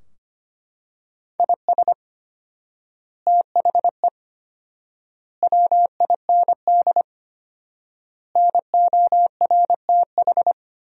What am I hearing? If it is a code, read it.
Morse code, heard as a keyed tone, about 25 wpm, 700 Hz. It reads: IS THE WIND NORTH
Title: 4x Word-Spacing - Sentences from Top 400 Words - 25wpm